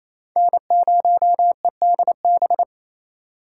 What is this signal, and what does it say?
Morse code, heard as a keyed tone, about 28 wpm, 700 Hz. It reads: N0EDB
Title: USA Call Signs - 5 Characters - 4x Character Spacing - 28wpm